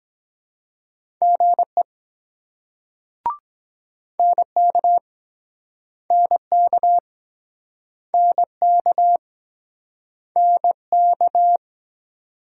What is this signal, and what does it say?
Morse code, heard as a keyed tone, about 24 wpm, 700 Hz. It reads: GE NK NK NK NK